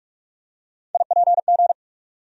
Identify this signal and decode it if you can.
Morse code, heard as a keyed tone, about 46 wpm, 700 Hz. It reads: IPG
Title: USA Call Sign Postfixes - 3 Letters 45wpm